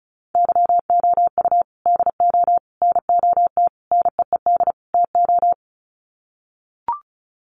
Morse code, heard as a keyed tone, about 34 wpm, 700 Hz. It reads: YOU DO NOT NEED TO
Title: Sets of 5 Words 35wpm